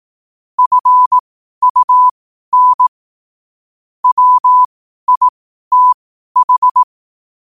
Morse code, heard as a keyed tone, about 18 wpm, spaced as for 12 wpm, 1000 Hz. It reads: FUN WITH
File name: qotdfiles / 12wpm